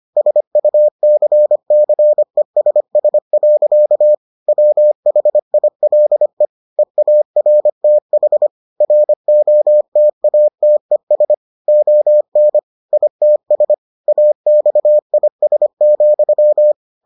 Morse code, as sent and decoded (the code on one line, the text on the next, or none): ... ..- -.-. -.-. ... ... .-.-.- / .-- .... .. .-.. / . .- .-. - .... / .-. --- - .- - . ... / --- -. / .. - ... / .- -..- .. ... --..--
SUCCESS. WHILE EARTH ROTATES ON ITS AXIS,